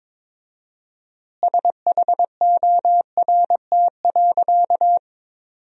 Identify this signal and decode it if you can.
Morse code, heard as a keyed tone, about 22 wpm, 700 Hz. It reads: SHORT.
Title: 4x Word-Spacing - Sentences from Top 500 Words - 22wpm